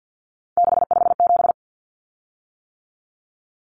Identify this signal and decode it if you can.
Morse code, heard as a keyed tone, about 50 wpm, 700 Hz. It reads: 657
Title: Common Cut Numbers - 3-Digit 50wpm